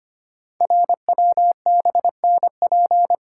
Morse code, heard as a keyed tone, about 25 wpm, 700 Hz. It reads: RWBNP
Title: Code Groups - 5 Letters - 25wpm